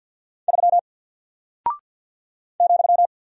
Morse code, heard as a keyed tone, about 48 wpm, 700 Hz. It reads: 3 ,
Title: Single Character (Letter:Number:Punctuation:Prosign) 50wpm